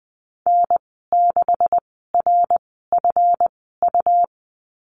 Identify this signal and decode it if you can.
Morse code, heard as a keyed tone, about 20 wpm, spaced as for 15 wpm, 700 Hz. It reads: N6RFU
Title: 154 Lesson 25g - Letter p - Call Signs 20wpm